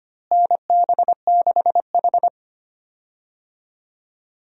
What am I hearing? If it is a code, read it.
Morse code, heard as a keyed tone, about 25 wpm, 700 Hz. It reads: NB6H
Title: USA Call Signs - 4 Characters - Repeated 3x - 25wpm